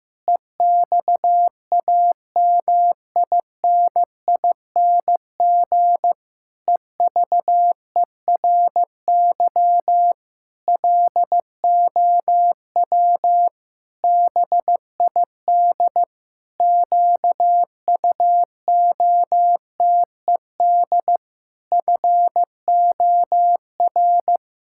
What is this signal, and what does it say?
Morse code, heard as a keyed tone, about 15 wpm, 700 Hz. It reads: EXAMINING EVERY LOW BID QUOTED FOR
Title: Pangram Challenge v2 - 15wpm